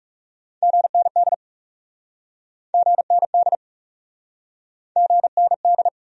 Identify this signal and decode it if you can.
Morse code, heard as a keyed tone, about 40 wpm, 700 Hz. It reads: GND GND GND